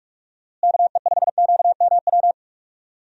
Morse code, heard as a keyed tone, about 46 wpm, 700 Hz. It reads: KE5QMW
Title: USA Call Signs - 6 Characters - Repeated 3x - 45wpm